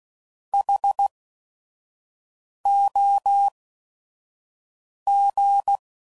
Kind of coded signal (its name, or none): Morse code